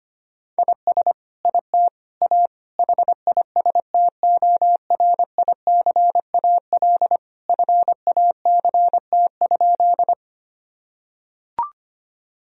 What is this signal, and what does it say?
Morse code, heard as a keyed tone, about 25 wpm, 700 Hz. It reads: IS IT A HISTORICAL FACT?
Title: Sentences from Top 1000 Words - Review Part 4 - 25wpm